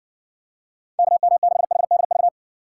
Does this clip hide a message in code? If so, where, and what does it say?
Yes: Morse code from 1.0 to 2.3 s, heard as a keyed tone, about 60 wpm, 700 Hz. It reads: KM6HDV